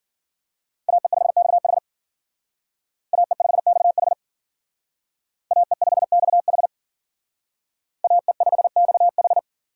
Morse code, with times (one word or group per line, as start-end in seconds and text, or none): 0.9-1.8 AE5XH
3.1-4.1 AE5XH
5.5-6.7 AE5XH
8.0-9.4 AE5XH